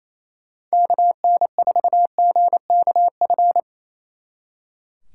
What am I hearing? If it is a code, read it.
Morse code, heard as a keyed tone, about 34 wpm, 700 Hz. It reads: KN4GKF
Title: Speed Racing - USA 2x3 Call Signs 28wpm